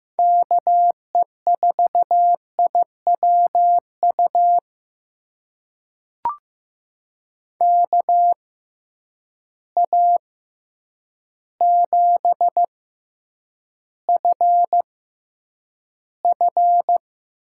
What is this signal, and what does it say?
Morse code, heard as a keyed tone, about 15 wpm, spaced as for 8 wpm, 700 Hz. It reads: KE4IWU KA7FF
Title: USA Call Signs - 6x Character Spacing - 15wpm